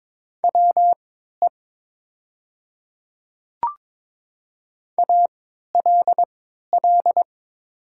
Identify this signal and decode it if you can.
Morse code, heard as a keyed tone, about 22 wpm, spaced as for 12 wpm, 700 Hz. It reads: WE ALL
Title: 3x Character Spacing - Top 300 Words - 22wpm